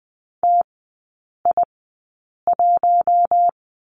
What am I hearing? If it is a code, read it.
Morse code, heard as a keyed tone, about 20 wpm, spaced as for 8 wpm, 700 Hz. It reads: TI1
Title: 100 Lesson 17e - Letter u - Three Characters 20wpm